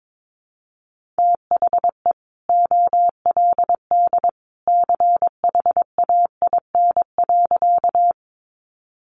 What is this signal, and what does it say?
Morse code, heard as a keyed tone, about 22 wpm, 700 Hz. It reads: THE OLD CHAIN.
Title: Sentences from Top 1500 Words 22wpm